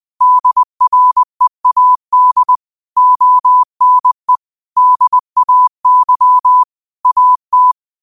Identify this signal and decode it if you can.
Morse code, heard as a keyed tone, about 20 wpm, 1000 Hz. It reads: DREAD ONE DAY AT